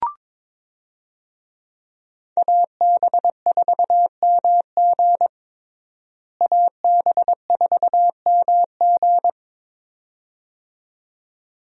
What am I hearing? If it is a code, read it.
Morse code, heard as a keyed tone, about 22 wpm, 700 Hz. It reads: AB4MG AB4MG
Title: USA Call Signs - 5 Characters - Repeated 2x - 22wpm